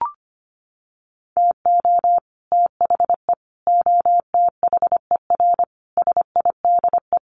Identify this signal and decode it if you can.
Morse code, heard as a keyed tone, about 25 wpm, 700 Hz. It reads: TO THE OTHER SIDE
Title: Sets of 5 Words Encore 25wpm